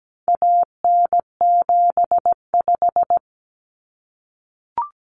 Morse code, heard as a keyed tone, about 17 wpm, 700 Hz. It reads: AN75
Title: Common Cut Numbers - 4-Digit 17wpm